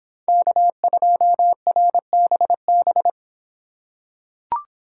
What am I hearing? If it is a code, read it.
Morse code, heard as a keyed tone, about 26 wpm, 700 Hz. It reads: K2RBB